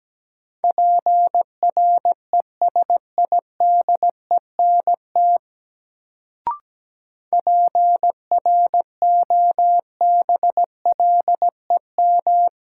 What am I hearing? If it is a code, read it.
Morse code, heard as a keyed tone, about 17 wpm, 700 Hz. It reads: PRESIDENT PROBLEM
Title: Sentences from Top 1000 Words - New Word Drill - 17wpm